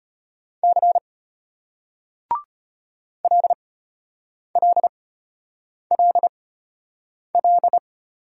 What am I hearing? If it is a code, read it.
Morse code, heard as a keyed tone, about 32 wpm, spaced as for 8 wpm, 700 Hz. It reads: C LLLL